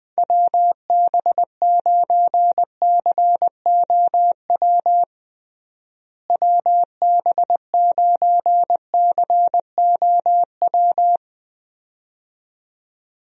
Morse code, heard as a keyed tone, about 21 wpm, 700 Hz. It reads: WB9COW WB9COW
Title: USA Call Signs - 6 Characters - Repeated 2x - 20wpm